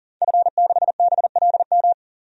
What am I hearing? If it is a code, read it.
Morse code, heard as a keyed tone, about 40 wpm, 700 Hz. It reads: F6BLM
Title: DXCC-100 Call Signs - 5 Characters - 40wpm